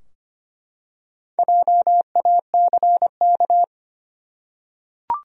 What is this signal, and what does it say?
Morse code, heard as a keyed tone, about 25 wpm, 700 Hz. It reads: JACK
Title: USA Common Names - Max 5 Letters - 25wpm